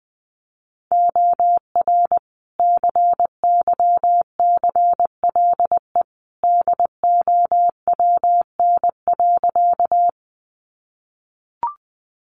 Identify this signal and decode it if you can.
Morse code, heard as a keyed tone, about 20 wpm, 700 Hz. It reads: OR CYCLE DOWN.
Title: Sentences from Top 900 Words 20wpm